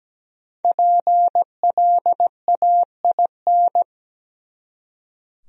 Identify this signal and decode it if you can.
Morse code, heard as a keyed tone, about 20 wpm, 700 Hz. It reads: PLAIN
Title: Speed-Racing Top 500 Words 17wpm